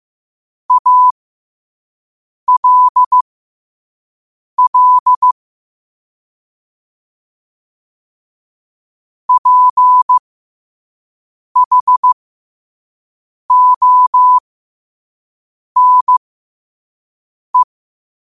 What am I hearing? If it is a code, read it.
Morse code, heard as a keyed tone, about 15 wpm, spaced as for 5 wpm, 1000 Hz. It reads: ALL PHONE